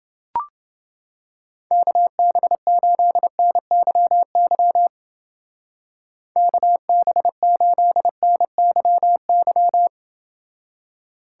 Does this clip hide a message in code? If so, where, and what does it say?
Yes: Morse code from 1.7 to 9.9 s, heard as a keyed tone, about 29 wpm, 700 Hz. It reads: KB8NYY KB8NYY